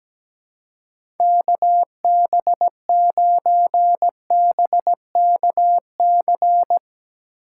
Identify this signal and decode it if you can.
Morse code, heard as a keyed tone, about 17 wpm, 700 Hz. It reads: KB9BKC
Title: USA Call Signs - 6 Characters - 4x Character Spacing - 17wpm